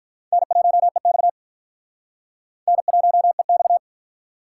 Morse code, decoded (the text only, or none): N1EX N1EX